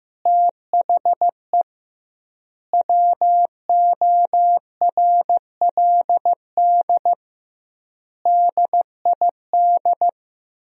Morse code, heard as a keyed tone, about 16 wpm, 700 Hz. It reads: THE WORLD DID
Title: 2x Word-Spacing - Sentences from Top 200 Words - 15wpm